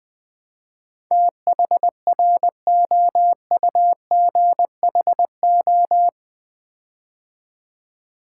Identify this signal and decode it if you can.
Morse code, heard as a keyed tone, about 20 wpm, 700 Hz. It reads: THROUGHO
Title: Build A Word - 4 Letters At A Time - 10-Letter Words - 20wpm